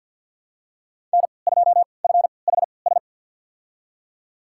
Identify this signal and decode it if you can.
Morse code, heard as a keyed tone, about 47 wpm, spaced as for 28 wpm, 700 Hz. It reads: N2FHS
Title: USA Call Signs - 5 Characters - 3x Character Spacing - 50wpm